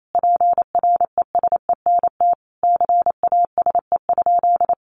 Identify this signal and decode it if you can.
Morse code, heard as a keyed tone, about 28 wpm, 700 Hz. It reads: PRESENT CASE?
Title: Sentences from Top 1200 Words 28wpm